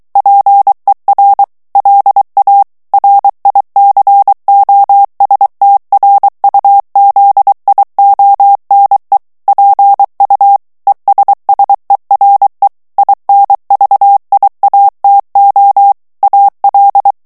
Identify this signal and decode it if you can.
Morse code, heard as a keyed tone, about 23 wpm, 800 Hz. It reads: PER LA RICOSTRUZIONE PU ESSERE INVIATO AL
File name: ITALIEN texte clair 1 vitesse 22 mots minute